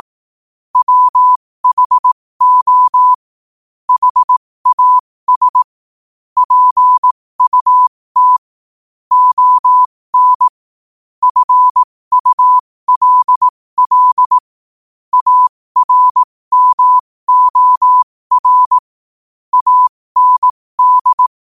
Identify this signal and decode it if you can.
Morse code, heard as a keyed tone, about 18 wpm, spaced as for 15 wpm, 1000 Hz. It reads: WHO HAS PUT ON FULL ARMOR AND
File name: qotdfiles / 15wpm